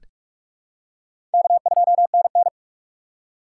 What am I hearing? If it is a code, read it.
Morse code, heard as a keyed tone, about 46 wpm, 700 Hz. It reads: K2NN
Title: USA Call Signs - 4 Characters - 2x Character Spacing - 45wpm